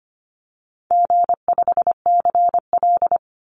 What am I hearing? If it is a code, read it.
Morse code, heard as a keyed tone, about 25 wpm, 700 Hz. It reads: G5CL
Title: Call Signs Repeated 2x - 25wpm